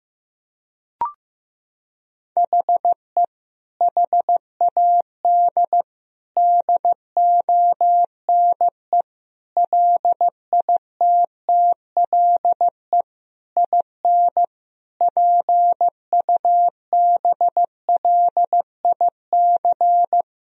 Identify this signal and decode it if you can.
Morse code, heard as a keyed tone, about 15 wpm, 700 Hz. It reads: HE HAD DONE LITTLE IN PUBLIC
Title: Sentences from Top 1000 Words - Review Part 3 - 15wpm